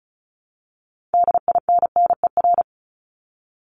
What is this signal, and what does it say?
Morse code, heard as a keyed tone, about 34 wpm, 700 Hz. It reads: DINNER
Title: Top 1751-2000 Words 35wpm